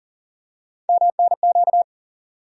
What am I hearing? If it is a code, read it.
Morse code, heard as a keyed tone, about 40 wpm, 700 Hz. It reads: MNQ
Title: USA Call Sign Postfixes - 3 Letters 40wpm